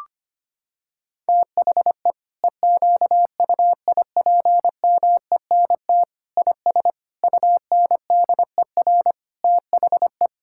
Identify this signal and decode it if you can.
Morse code, heard as a keyed tone, about 25 wpm, 700 Hz. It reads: THE EQUIPMENT IS UNDER THE
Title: Sentences from Top 1000 Words - Review Part 5 - 25wpm